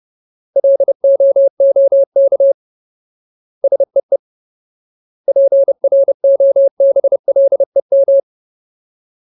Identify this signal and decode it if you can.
Morse code, heard as a keyed tone, about 30 wpm, 550 Hz. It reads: LOOK SEE PROBLEM
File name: Ultimate Word List / 30WPM Ultimate